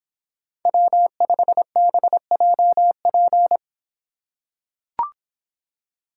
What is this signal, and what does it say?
Morse code, heard as a keyed tone, about 25 wpm, 700 Hz. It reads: W5BJP